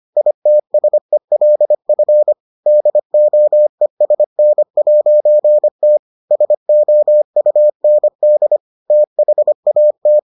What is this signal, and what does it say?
Morse code, heard as a keyed tone, about 25 wpm, 600 Hz. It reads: ITSELF DOESN'T SOUND THAT